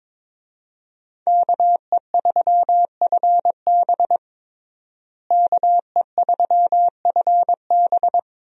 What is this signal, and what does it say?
Morse code, heard as a keyed tone, about 22 wpm, 700 Hz. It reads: KE3FB KE3FB
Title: USA Call Signs - 5 Characters - Repeated 3x - 22wpm